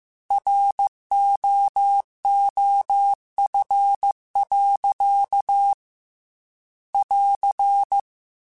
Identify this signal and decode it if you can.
Morse code, heard as a keyed tone, about 15 wpm, 800 Hz. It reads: ROOF. <AR>
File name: Weather Sets / Weather Sets-mp3s / Weather 4 28 @13